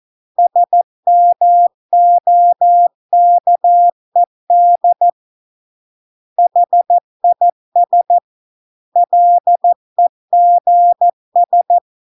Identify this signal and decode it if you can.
Morse code, heard as a keyed tone, about 14 wpm, 700 Hz. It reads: SMOKED HIS LEGS